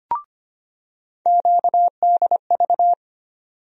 Morse code, heard as a keyed tone, about 25 wpm, 700 Hz. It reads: QDV
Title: USA Call Sign Postfixes - 3 Letters 25wpm